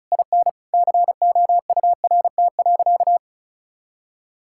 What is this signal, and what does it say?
Morse code, heard as a keyed tone, about 34 wpm, 700 Hz. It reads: IN COURT.
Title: Sentences from Top 1600 Words 35wpm